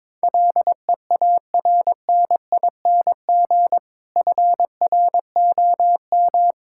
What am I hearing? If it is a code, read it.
Morse code, heard as a keyed tone, about 22 wpm, 700 Hz. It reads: LEARNING FROM
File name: Difficult Sentences with Hints - 22wpm